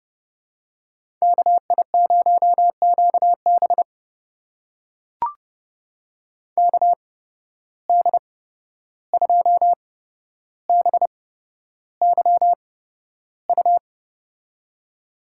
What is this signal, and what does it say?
Morse code, heard as a keyed tone, about 32 wpm, spaced as for 12 wpm, 700 Hz. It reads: KI0QB KD2BYU
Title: USA Call Signs - 8x Character Spacing - 30wpm